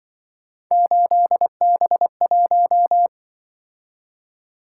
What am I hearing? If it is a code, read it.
Morse code, heard as a keyed tone, about 25 wpm, 700 Hz. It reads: 8B1
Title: Speed-Racing - Three Letter-Number Combinations - 20wpm